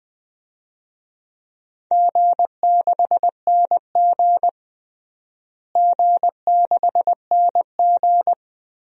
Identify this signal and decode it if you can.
Morse code, heard as a keyed tone, about 20 wpm, 700 Hz. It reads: G6NG G6NG
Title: DXCC-100 Call Signs - 4 Characters - Repeated 2x - 20wpm